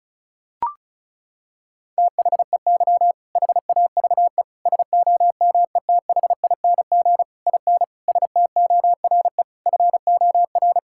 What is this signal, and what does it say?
Morse code, heard as a keyed tone, about 36 wpm, 700 Hz. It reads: THEY HAVE SOMETHING IN STORE FOR
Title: Sentences from Top 1000 Words - Review Part 5 - 35wpm